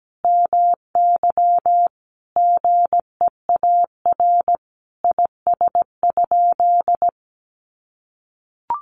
Morse code, heard as a keyed tone, about 17 wpm, 700 Hz. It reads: MY GEAR IS?
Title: Sentences from Top 1400 Words 17wpm